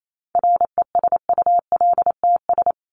Morse code, heard as a keyed tone, about 28 wpm, 700 Hz. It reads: RESULTS
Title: ES:S Suffix - Top 100 Words with Base Words 28wpm